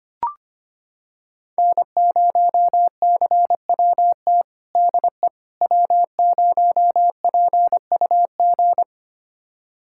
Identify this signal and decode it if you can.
Morse code, heard as a keyed tone, about 25 wpm, 700 Hz. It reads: N0CWT DE W0PUG
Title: USA Callsign DE Callsign - 5 Characters - 25wpm